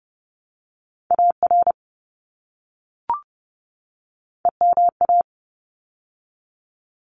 Morse code, heard as a keyed tone, about 30 wpm, 700 Hz. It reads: AR EMA
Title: ARRL Sections 30wpm